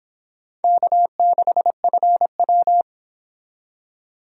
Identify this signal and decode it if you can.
Morse code, heard as a keyed tone, about 30 wpm, 700 Hz. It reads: K6FW